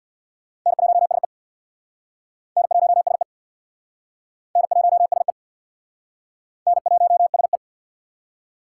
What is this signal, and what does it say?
Morse code, heard as a keyed tone, about 67 wpm, 700 Hz. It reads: N1SE N1SE N1SE N1SE